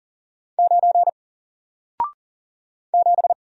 Morse code, heard as a keyed tone, about 39 wpm, 700 Hz. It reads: 9 7
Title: Single Character (Letter:Number:Punctuation:Prosign) Alt 40wpm